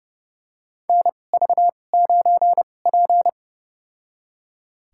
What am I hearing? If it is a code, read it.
Morse code, heard as a keyed tone, about 30 wpm, spaced as for 22 wpm, 700 Hz. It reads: NV9P
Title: USA Call Signs - 2x Character Spacing - 30wpm